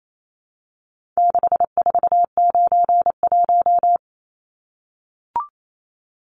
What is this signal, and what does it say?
Morse code, heard as a keyed tone, about 28 wpm, 700 Hz. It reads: 6491
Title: Numbers - 4-Digit 28wpm